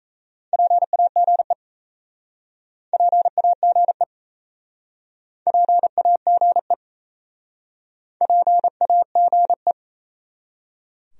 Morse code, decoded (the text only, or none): PAGE PAGE PAGE PAGE